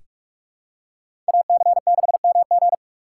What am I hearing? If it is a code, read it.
Morse code, heard as a keyed tone, about 46 wpm, 700 Hz. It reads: AC6MG
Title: USA Call Signs - 5 Characters - 8x Character Spacing - 45wpm